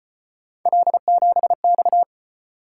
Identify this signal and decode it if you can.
Morse code, heard as a keyed tone, about 33 wpm, 700 Hz. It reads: L7X